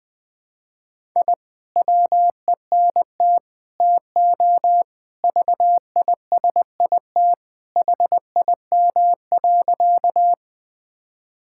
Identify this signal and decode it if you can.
Morse code, heard as a keyed tone, about 21 wpm, 700 Hz. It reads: I WENT TO VISIT HIM.
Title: Sentences from Top 1000 Words - Review Part 6 - 20wpm